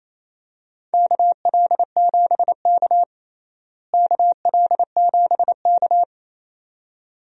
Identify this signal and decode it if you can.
Morse code, heard as a keyed tone, about 28 wpm, 700 Hz. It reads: KL7K KL7K
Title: DXCC-100 Call Signs - 4 Characters - Repeated 2x - 28wpm